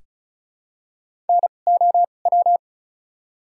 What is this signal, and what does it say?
Morse code, heard as a keyed tone, about 34 wpm, spaced as for 25 wpm, 700 Hz. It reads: NOW
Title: 2x Character Spacing - Top 300 Words - 35wpm